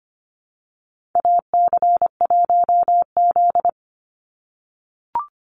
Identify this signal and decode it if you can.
Morse code, heard as a keyed tone, about 25 wpm, 700 Hz. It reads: AC1Z
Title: NA SOTA Chaser Callsigns 25wpm